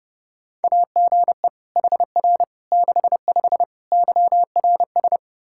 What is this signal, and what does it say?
Morse code, heard as a keyed tone, about 32 wpm, 700 Hz. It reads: AGE HR 65 YRS
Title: Practice QSO - Exchange 3 - More Info - 30wpm